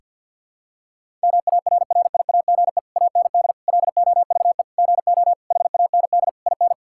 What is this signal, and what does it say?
Morse code, heard as a keyed tone, about 52 wpm, 690 Hz. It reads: MARRIAGE AND LOVE GO HAND IN
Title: Sentences from Top 1000 Words - Review Part 5 - 50wpm